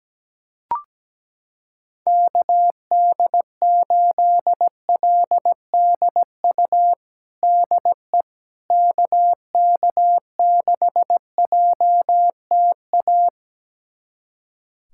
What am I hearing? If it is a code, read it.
Morse code, heard as a keyed tone, about 17 wpm, 700 Hz. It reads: KD8LDU DE KK6JTA
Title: USA Callsign DE Callsign - 6 Characters - 17wpm